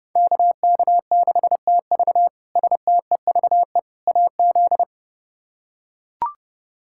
Morse code, heard as a keyed tone, about 31 wpm, 700 Hz. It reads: KK6TV STEVE AZ
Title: K1USN SST QSO Practice - 30wpm